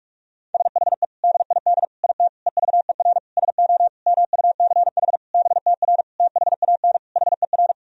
Morse code, heard as a keyed tone, about 46 wpm, 700 Hz. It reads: SHE DID IT EVER SO MUCH BTR THAN HER
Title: Sentences from Top 400 Words and Abbreviations 45wpm